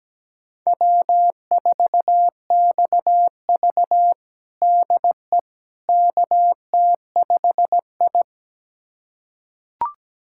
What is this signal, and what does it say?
Morse code, heard as a keyed tone, about 17 wpm, 700 Hz. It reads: W4XV DE KT5I
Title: USA Callsign DE Callsign - 4 Characters - 17wpm